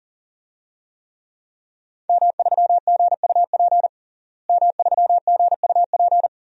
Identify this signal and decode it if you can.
Morse code, heard as a keyed tone, about 40 wpm, 700 Hz. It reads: M3GUP M3GUP
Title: DXCC-100 Call Signs - 5 Characters - Repeated 2x - 40wpm